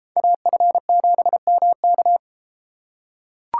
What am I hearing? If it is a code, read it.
Morse code, heard as a keyed tone, about 32 wpm, 700 Hz. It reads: AF7MK